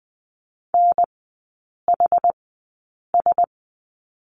Morse code, heard as a keyed tone, about 20 wpm, spaced as for 8 wpm, 700 Hz. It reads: NHS
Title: 060 Lesson 11e - Letter h - Three Characters 20wpm